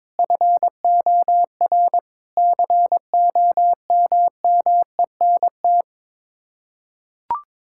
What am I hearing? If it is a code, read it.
Morse code, heard as a keyed tone, about 22 wpm, 700 Hz. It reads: FOR COMMENT
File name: practice / 22wpm / Sets of 6 Words - 22wpm